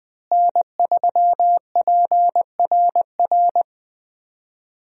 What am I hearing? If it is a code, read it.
Morse code, heard as a keyed tone, about 20 wpm, 700 Hz. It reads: N3PRR
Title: USA Call Signs - 5 Characters - Repeated 3x - 20wpm